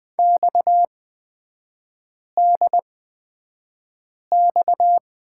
Morse code, heard as a keyed tone, about 20 wpm, spaced as for 5 wpm, 690 Hz. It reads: XDX
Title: 229 Lesson 37e - Letter x - Three Characters 20wpm